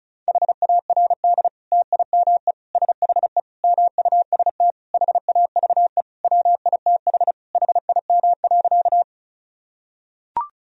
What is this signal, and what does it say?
Morse code, heard as a keyed tone, about 36 wpm, 700 Hz. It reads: HARD TIME SHE MUST HAVE WITH HIM.
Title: Sentences from Top 1000 Words - Review Part 1 - 35wpm